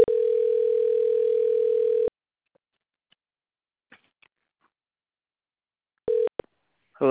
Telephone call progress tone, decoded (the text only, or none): ringback tone